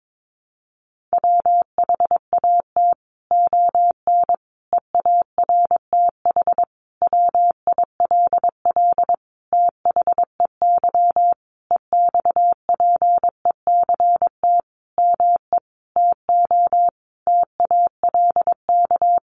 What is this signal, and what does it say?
Morse code, heard as a keyed tone, about 22 wpm, 700 Hz. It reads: WHAT ON EARTH WILL THEY EXPECT ME TO TALK
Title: Sentences from Top 1100 Words 22wpm